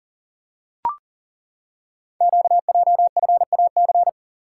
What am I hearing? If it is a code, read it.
Morse code, heard as a keyed tone, about 42 wpm, 690 Hz. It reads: QJFAC
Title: Code Groups - 5 Letters - 40wpm